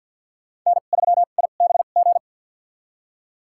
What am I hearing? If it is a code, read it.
Morse code, heard as a keyed tone, about 52 wpm, spaced as for 37 wpm, 700 Hz. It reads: N3IBG
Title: USA Call Signs - 2x Character Spacing - 50wpm